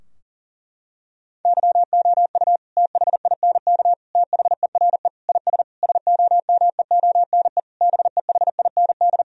Morse code, heard as a keyed tone, about 40 wpm, 700 Hz. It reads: YOU THINK THERE IS SOMEONE BEHIND
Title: Sentences from Top 1000 Words - Review Part 4 - 40wpm